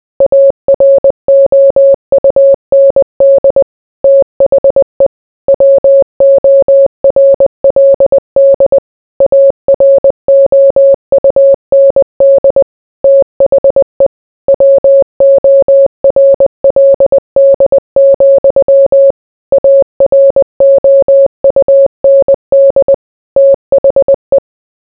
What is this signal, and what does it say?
Morse code, heard as a keyed tone, about 20 wpm, 550 Hz. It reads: AROUND THE WORLD AROUND THE WORLD, AROUND THE